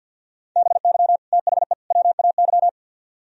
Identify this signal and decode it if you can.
Morse code, heard as a keyed tone, about 65 wpm, 700 Hz. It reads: BY THE WAY